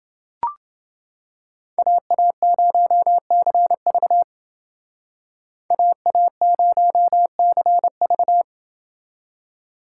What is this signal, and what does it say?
Morse code, heard as a keyed tone, about 28 wpm, 700 Hz. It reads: AA0CV AA0CV